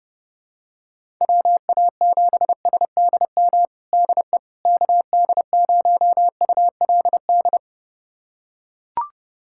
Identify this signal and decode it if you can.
Morse code, heard as a keyed tone, about 31 wpm, 700 Hz. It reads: WA7SDM DE KD0ULD
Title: USA Callsign DE Callsign - 6 Characters - 30wpm